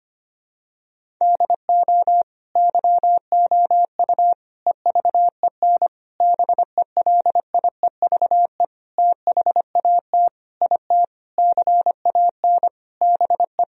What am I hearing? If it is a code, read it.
Morse code, heard as a keyed tone, about 25 wpm, 700 Hz. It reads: DO YOU EVEN BELIEVE THAT IT CAN BE
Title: Sentences from Top 1000 Words - Review Part 1 - 25wpm